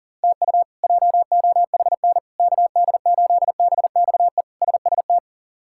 Morse code, heard as a keyed tone, about 40 wpm, 690 Hz. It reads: TU JOHN KD8BXE SST
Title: K1USN SST QSO Practice - 40wpm